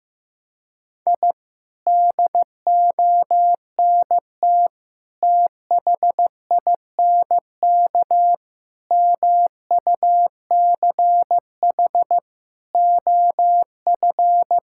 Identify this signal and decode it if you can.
Morse code, heard as a keyed tone, about 16 wpm, 700 Hz. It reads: I DONT THINK MUCH OF
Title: Sentences from Top 1000 Words - Review Part 6 - 15wpm